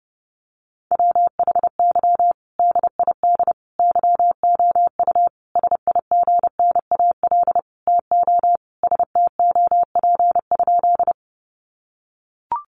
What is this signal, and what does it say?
Morse code, heard as a keyed tone, about 30 wpm, 700 Hz. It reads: WHY DID YOU SIGNAL TO STOP?
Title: Sentences from Top 1400 Words 30wpm